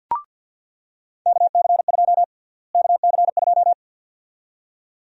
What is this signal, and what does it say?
Morse code, heard as a keyed tone, about 48 wpm, 700 Hz. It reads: KC2 KC2
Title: USA Call Sign Prefixes - Repeated 2x - 50wpm